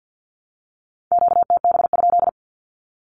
Morse code, heard as a keyed tone, about 48 wpm, 700 Hz. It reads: QTB?
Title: Q-Codes 50wpm